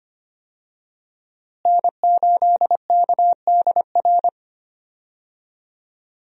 Morse code, heard as a keyed tone, about 25 wpm, 700 Hz. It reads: N8KDR
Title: USA Call Signs - 5 Characters - Repeated 3x - 25wpm